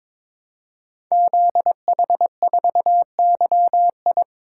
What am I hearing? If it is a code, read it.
Morse code, heard as a keyed tone, about 22 wpm, 700 Hz. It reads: ZH4YI
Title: Code Groups - 5 Letter:Numbers - 22wpm